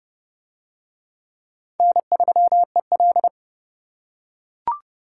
Morse code, heard as a keyed tone, about 30 wpm, 700 Hz. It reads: N3EL